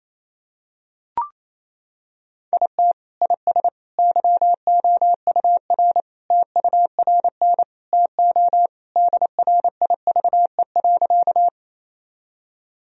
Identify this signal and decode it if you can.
Morse code, heard as a keyed tone, about 28 wpm, 700 Hz. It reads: IT IS YOUR TURN TO DRIVE.
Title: Sentences from Top 1000 Words - Review Part 2 - 28wpm